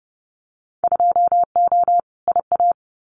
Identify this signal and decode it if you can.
Morse code, heard as a keyed tone, about 30 wpm, 700 Hz. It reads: 2O IA
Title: Winter Field Day Exchanges 30wpm